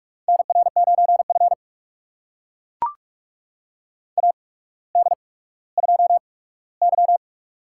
Morse code, heard as a keyed tone, about 44 wpm, spaced as for 19 wpm, 700 Hz. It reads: NR9F AD2Y
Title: USA Call Signs - 4 Characters - 8x Character Spacing - 45wpm